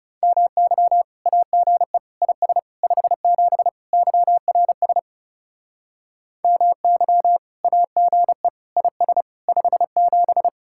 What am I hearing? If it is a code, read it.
Morse code, heard as a keyed tone, about 37 wpm, 700 Hz. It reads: MY AGE IS 57 YRS MY AGE IS 57
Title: Speed-Racing Common QSO Phrases Encore - 30wpm